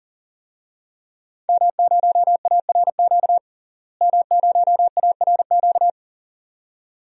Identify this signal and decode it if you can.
Morse code, heard as a keyed tone, about 40 wpm, 700 Hz. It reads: M0ARQ M0ARQ
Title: DXCC-100 Call Signs - 5 Characters - Repeated 2x - 40wpm